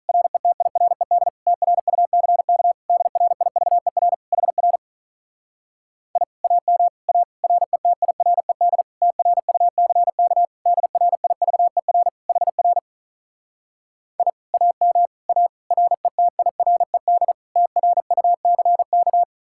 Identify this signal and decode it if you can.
Morse code, heard as a keyed tone, about 46 wpm, 700 Hz. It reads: RETIRED TRUCK DRIVER HR I AM A RETIRED TRUCK DRIVER HR I AM A RETIRED TRUCK